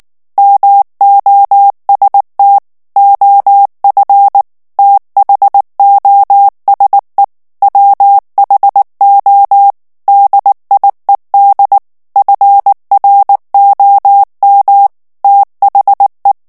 Morse code, heard as a keyed tone, about 19 wpm, 800 Hz. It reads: MOST OF THOSE WHO DIED FROM THE
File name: morse mp3 / ANGLAIS texte clair 2 vitesse 18 mots minute